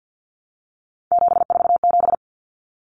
Morse code, heard as a keyed tone, about 50 wpm, 700 Hz. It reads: 747
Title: Numbers - 3-Digit 50wpm